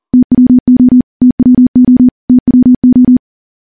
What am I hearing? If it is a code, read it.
Morse code, heard as a keyed tone, about 42 wpm, 250 Hz. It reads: TWO TWO TWO
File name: morse / morse-training-40WPM